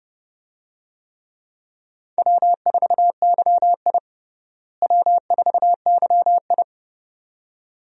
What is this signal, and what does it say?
Morse code, heard as a keyed tone, about 32 wpm, 700 Hz. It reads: W4YI W4YI
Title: USA Call Signs - 4 Characters - Repeated 2x - 30wpm